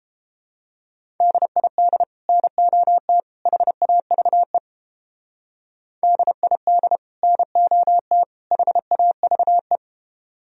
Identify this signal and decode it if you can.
Morse code, heard as a keyed tone, about 29 wpm, 700 Hz. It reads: DID NOT HAVE DID NOT HAVE